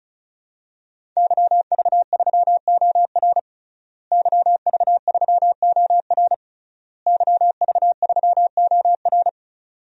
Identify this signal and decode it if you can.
Morse code, heard as a keyed tone, about 36 wpm, 700 Hz. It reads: YV3OR YV3OR YV3OR
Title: DXCC-100 Call Signs - 5 Characters - Repeated 3x - 35wpm